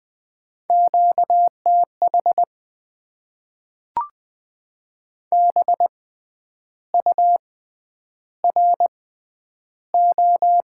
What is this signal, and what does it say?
Morse code, heard as a keyed tone, about 20 wpm, spaced as for 9 wpm, 700 Hz. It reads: QTH BURO
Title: Top 90 QSO Elements - 6x Character Spacing - 20wpm